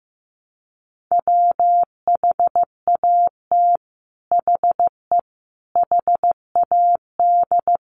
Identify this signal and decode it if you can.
Morse code, heard as a keyed tone, about 15 wpm, 700 Hz. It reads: WHAT HE HAD
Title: Sets of 5 Words Encore2 15wpm